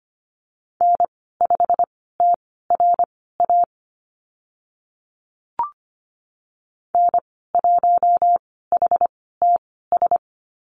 Morse code, heard as a keyed tone, about 25 wpm, spaced as for 16 wpm, 700 Hz. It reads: N5TRA N1HTS
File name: Lesson 15g - Number 5 - Call Signs 25wpm